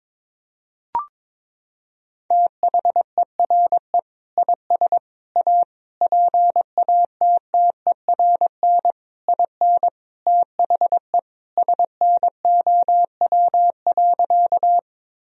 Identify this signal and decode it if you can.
Morse code, heard as a keyed tone, about 22 wpm, 700 Hz. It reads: THERE IS A PATTERN IN THE SNOW.
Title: Sentences from Top 1000 Words - Review Part 2 - 22wpm